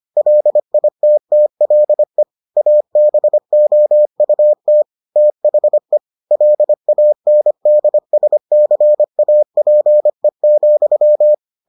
Morse code, heard as a keyed tone, about 25 wpm, 600 Hz. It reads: LITTLE ABOUT THE LANDSCAPE,